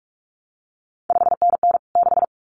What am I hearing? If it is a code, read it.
Morse code, heard as a keyed tone, about 46 wpm, 700 Hz. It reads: 5NN 6
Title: CQ WW DX Contest Exchange Encore - 45wpm